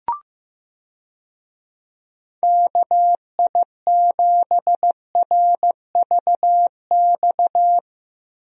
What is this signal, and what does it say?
Morse code, heard as a keyed tone, about 15 wpm, 700 Hz. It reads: KI7RVX
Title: USA Call Signs - 6 Characters - Repeated 2x - 15wpm